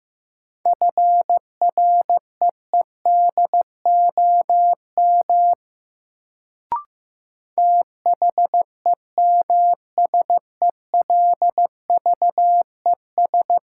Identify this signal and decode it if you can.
Morse code, heard as a keyed tone, about 16 wpm, 700 Hz. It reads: FREEDOM THEMSELVES
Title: Sentences from Top 800 Words - New Word Drill - 15wpm